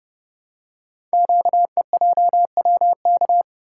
Morse code, heard as a keyed tone, about 30 wpm, 690 Hz. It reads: QEJWK
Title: Code Groups - 5 Letters - 30wpm